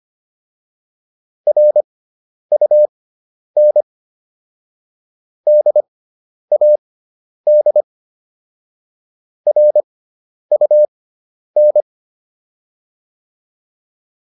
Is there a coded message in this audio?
Yes: Morse code from 1.5 to 11.8 s, heard as a keyed tone, about 25 wpm, spaced as for 10 wpm, 600 Hz. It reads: RUN DAD RUN